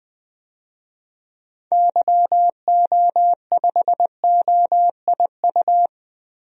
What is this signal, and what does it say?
Morse code, heard as a keyed tone, about 20 wpm, 700 Hz. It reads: YO5OIU
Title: DXCC-100 Call Signs - 6 Characters - Repeated 3x - 20wpm